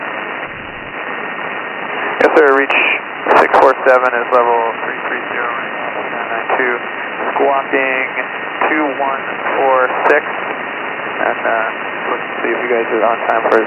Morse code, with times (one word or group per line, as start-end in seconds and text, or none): none